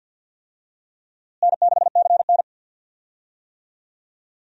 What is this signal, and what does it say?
Morse code, heard as a keyed tone, about 46 wpm, 700 Hz. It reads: N6CN
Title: USA Call Signs - 4 Characters - Repeated 3x - 50wpm